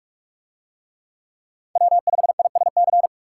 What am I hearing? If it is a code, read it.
Morse code, heard as a keyed tone, about 41 wpm, 700 Hz. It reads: W5ISC